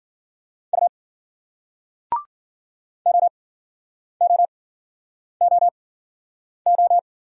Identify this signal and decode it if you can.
Morse code, heard as a keyed tone, about 48 wpm, spaced as for 9 wpm, 700 Hz. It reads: U OOOO
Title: Speed-Racing Single Letters 40wpm